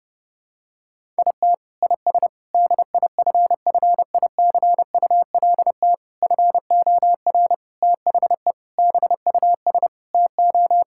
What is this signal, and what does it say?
Morse code, heard as a keyed tone, about 32 wpm, 700 Hz. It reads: IT IS DIFFICULT FOR THE BUS TO
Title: Sentences from Top 1000 Words - Review Part 6 - 30wpm